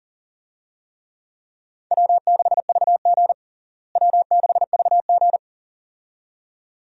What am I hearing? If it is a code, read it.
Morse code, heard as a keyed tone, about 40 wpm, 700 Hz. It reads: W6VG W6VG
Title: USA Call Signs - 4 Characters - Repeated 2x - 40wpm